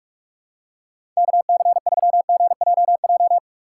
Morse code, heard as a keyed tone, about 46 wpm, 700 Hz. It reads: KC3GJJ
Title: USA Call Signs - 6 Characters - 4x Character Spacing - 45wpm